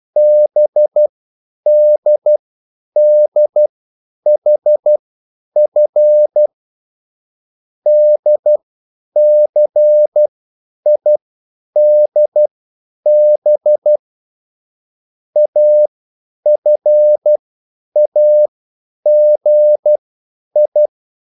Morse code, heard as a keyed tone, about 12 wpm, spaced as for 9 wpm, 600 Hz. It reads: BDDHF DCIDB AFAGI
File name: QSL / CW lesson